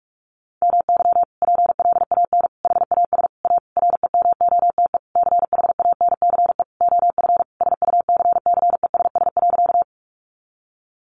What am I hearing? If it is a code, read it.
Morse code, heard as a keyed tone, about 46 wpm, 700 Hz. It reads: MY PLAN HAS A REMOTE CHANCE OF SUCCESS.
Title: Sentences from Top 1300 Words 45wpm